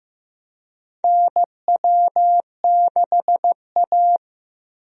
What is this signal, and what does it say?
Morse code, heard as a keyed tone, about 15 wpm, 700 Hz. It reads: NW6A
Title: USA Call Signs - 4 Characters - Repeated 2x - 15wpm